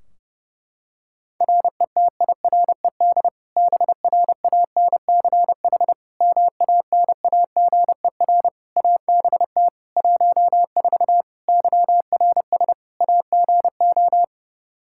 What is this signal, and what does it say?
Morse code, heard as a keyed tone, about 32 wpm, 700 Hz. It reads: RETIRED BRANCH MANAGER ABT 14 YRS AGO
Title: Practice QSO - Exchange 3 - More Info - 30wpm